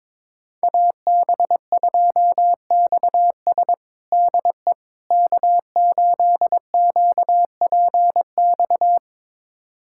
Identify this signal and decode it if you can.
Morse code, heard as a keyed tone, about 22 wpm, 700 Hz. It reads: AB2XS DE K8QPX
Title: USA Callsign DE Callsign - 5 Characters - 22wpm